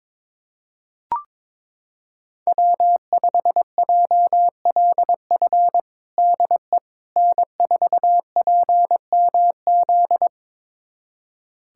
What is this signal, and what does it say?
Morse code, heard as a keyed tone, about 22 wpm, 700 Hz. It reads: W5JLF DE N4PMZ
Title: USA Callsign DE Callsign - 5 Characters - 22wpm